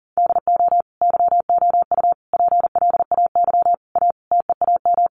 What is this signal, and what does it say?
Morse code, heard as a keyed tone, about 40 wpm, 700 Hz. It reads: DO YOU PLAY A TEAM
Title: Sentences from Top 1500 Words 40wpm